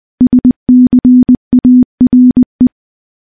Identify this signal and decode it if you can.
Morse code, heard as a keyed tone, about 20 wpm, 250 Hz. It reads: SCARE
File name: morse / morse-training-20WPM